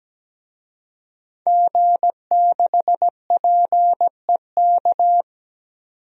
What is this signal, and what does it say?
Morse code, heard as a keyed tone, about 17 wpm, 700 Hz. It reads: G6PEK
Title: DXCC-100 Call Signs - 5 Characters - 17wpm